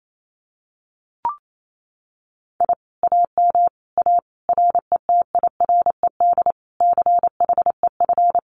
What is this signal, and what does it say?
Morse code, heard as a keyed tone, about 28 wpm, 700 Hz. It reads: I AM A RETIRED CHEF
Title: Common QSO Phrases Encore 28wpm